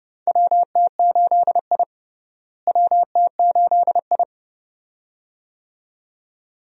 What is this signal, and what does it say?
Morse code, heard as a keyed tone, about 31 wpm, 700 Hz. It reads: WT8I WT8I
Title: USA Call Signs - 4 Characters - Repeated 2x - 30wpm